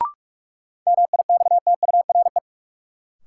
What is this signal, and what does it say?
Morse code, heard as a keyed tone, about 46 wpm, 700 Hz. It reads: MIXTURE
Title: Sentences from Top 1500 Words - New Word Drill - 45wpm